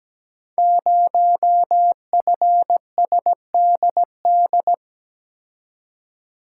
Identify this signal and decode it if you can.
Morse code, heard as a keyed tone, about 17 wpm, 690 Hz. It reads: 0FSDD
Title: Code Groups - 5 Letter:Numbers - 17wpm